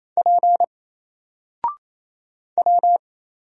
Morse code, heard as a keyed tone, about 27 wpm, 700 Hz. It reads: P W
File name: Single-Letter-Number-28wpm